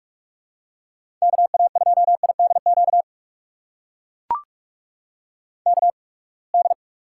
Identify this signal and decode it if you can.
Morse code, heard as a keyed tone, about 43 wpm, spaced as for 30 wpm, 700 Hz. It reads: KA2IDQ KD
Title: USA Call Signs - 6 Characters - 8x Character Spacing - 45wpm